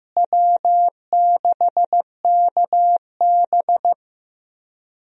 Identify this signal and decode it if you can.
Morse code, heard as a keyed tone, about 16 wpm, 700 Hz. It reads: W6KB
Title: USA Call Signs - 4 Characters - Repeated 3x - 15wpm